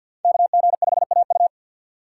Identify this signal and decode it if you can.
Morse code, heard as a keyed tone, about 46 wpm, 700 Hz. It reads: KG5AU
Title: USA Call Signs - 5 Characters - 4x Character Spacing - 50wpm